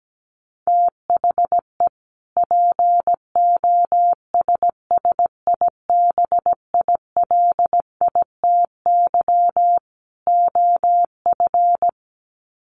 Morse code, heard as a keyed tone, about 17 wpm, 700 Hz. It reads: THE POSSIBILITY OF
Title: Sets of 3 Words 17wpm